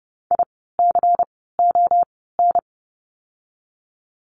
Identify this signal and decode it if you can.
Morse code, heard as a keyed tone, about 30 wpm, 700 Hz. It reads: I C O N
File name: Lesson 16f - Letter c - Words 30wpm